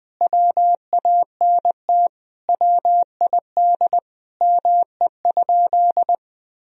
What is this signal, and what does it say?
Morse code, heard as a keyed tone, about 21 wpm, 700 Hz. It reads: WANT WID ME?
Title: Sentences from Top 200 Words and Abbreviations 20wpm